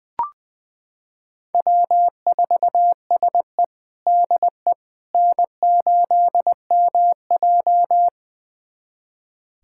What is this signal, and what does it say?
Morse code, heard as a keyed tone, about 20 wpm, 700 Hz. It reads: W4SE DE N8MJ
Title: USA Callsign DE Callsign - 4 Characters - 20wpm